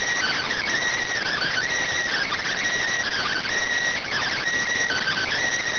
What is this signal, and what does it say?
Emergency Alert System (SAME) tones, heard as harsh buzzing bursts. It is heard from 1.2 to 5.3 s.